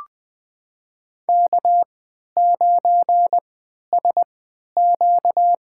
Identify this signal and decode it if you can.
Morse code, heard as a keyed tone, about 20 wpm, spaced as for 11 wpm, 700 Hz. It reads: K9SQ
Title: USA Call Signs - 4 Characters - 3x Character Spacing - 20wpm